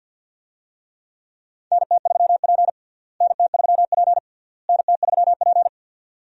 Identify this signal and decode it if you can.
Morse code, heard as a keyed tone, about 52 wpm, 700 Hz. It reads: NT3P NT3P NT3P
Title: USA Call Signs - 4 Characters - Repeated 3x - 50wpm